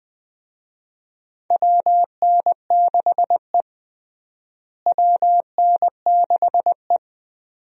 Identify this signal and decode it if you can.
Morse code, heard as a keyed tone, about 20 wpm, 700 Hz. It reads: WN6E WN6E
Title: USA Call Signs - 4 Characters - Repeated 2x - 20wpm